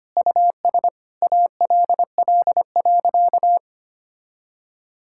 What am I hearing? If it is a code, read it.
Morse code, heard as a keyed tone, about 25 wpm, 700 Hz. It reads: US ALL.
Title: Sentences from Top 1000 Words - Review Part 5 - 25wpm